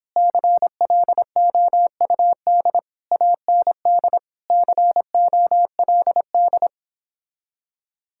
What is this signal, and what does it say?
Morse code, heard as a keyed tone, about 29 wpm, 700 Hz. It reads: CLOUD AND COLD